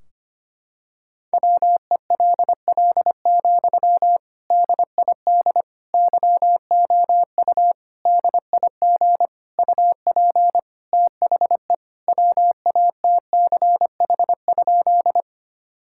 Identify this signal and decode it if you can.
Morse code, heard as a keyed tone, about 25 wpm, 700 Hz. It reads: WELL, DID YOU DIG UP THE WATCH?
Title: Sentences from Top 1600 Words 25wpm